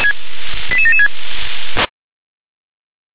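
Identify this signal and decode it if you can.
Five-tone selective call (ZVEI), heard as quick runs of five tones, tones of about 70 ms. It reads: 88776
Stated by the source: ZVEI five-tone selective call 88795 received by SDR.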